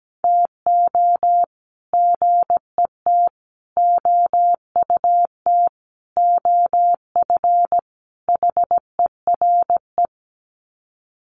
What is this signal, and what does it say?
Morse code, heard as a keyed tone, about 17 wpm, 700 Hz. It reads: TO GET OUT OF HERE
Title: Sets of 5 Words Encore 17wpm